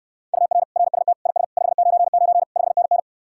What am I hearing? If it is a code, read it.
Morse code, heard as a keyed tone, about 69 wpm, 700 Hz. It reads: UR RST IS 599 5NN